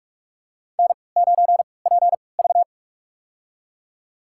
Morse code, heard as a keyed tone, about 44 wpm, spaced as for 25 wpm, 700 Hz. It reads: N9PV
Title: USA Call Signs - 4 Characters - 3x Character Spacing - 45wpm